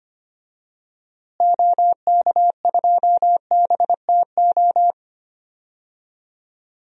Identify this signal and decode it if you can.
Morse code, heard as a keyed tone, about 25 wpm, 700 Hz. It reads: OK2BTO
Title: DXCC-100 Call Signs - 6 Characters - Repeated 2x - 25wpm